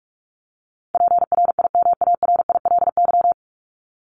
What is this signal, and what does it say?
Morse code, heard as a keyed tone, about 44 wpm, 700 Hz. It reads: PRIMARILY
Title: LY Suffix - Top 100 Words with Base Words 45wpm